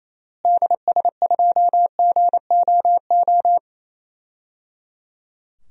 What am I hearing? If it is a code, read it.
Morse code, heard as a keyed tone, about 28 wpm, 700 Hz. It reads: DS2GOO
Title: DXCC-100 Call Signs - 6 Characters - Repeated 3x - 28wpm